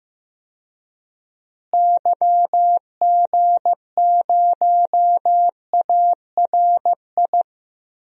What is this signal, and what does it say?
Morse code, heard as a keyed tone, about 15 wpm, 700 Hz. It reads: YG0ARI
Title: DXCC-100 Call Signs - 6 Characters - Repeated 3x - 15wpm